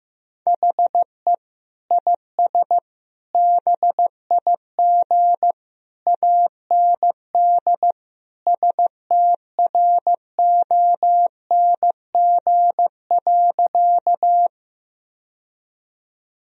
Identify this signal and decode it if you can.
Morse code, heard as a keyed tone, about 15 wpm, 700 Hz. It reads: HE IS BIG AND STRONG.
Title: Sentences from Top 1000 Words - Review Part 3 - 15wpm